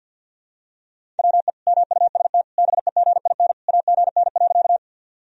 Morse code, heard as a keyed tone, about 52 wpm, 700 Hz. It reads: WE MUST BEGIN AGN.
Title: Sentences from Top 300 Words and Abbreviations 50wpm